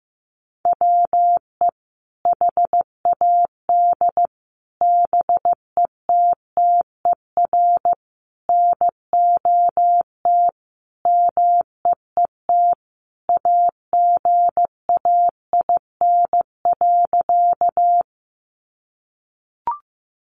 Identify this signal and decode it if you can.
Morse code, heard as a keyed tone, about 15 wpm, 700 Hz. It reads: WE HAD BETTER NOT MEET AGAIN.
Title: Sentences from Top 1100 Words 15wpm